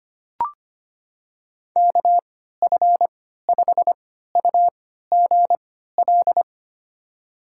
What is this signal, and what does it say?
Morse code, heard as a keyed tone, about 25 wpm, spaced as for 14 wpm, 700 Hz. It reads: KF5UGL
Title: USA Call Signs - 6 Characters - 3x Character Spacing - 25wpm